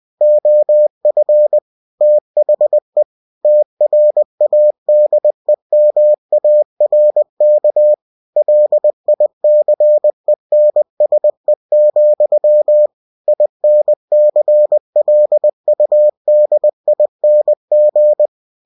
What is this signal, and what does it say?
Morse code, heard as a keyed tone, about 20 wpm, 600 Hz. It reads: OF THE TRADEMARK LICENSE, INCLUDING
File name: cw-books / winnie-the-pooh / winnie-20wpm-page-0112